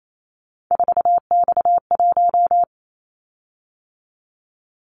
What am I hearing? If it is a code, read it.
Morse code, heard as a keyed tone, about 28 wpm, 700 Hz. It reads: 4X1
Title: Callsign Prefixes for Top 50 DXCC - 28wpm